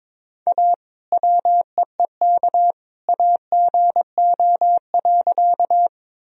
Morse code, heard as a keyed tone, about 22 wpm, 700 Hz. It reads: A WEEK AGO.
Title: Sentences from Top 1000 Words - Review Part 3 - 22wpm